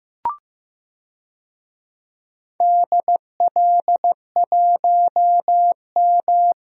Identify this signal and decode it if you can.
Morse code, heard as a keyed tone, about 15 wpm, 700 Hz. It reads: DL1M
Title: DXCC-100 Call Signs - 4 Characters - Repeated 2x - 15wpm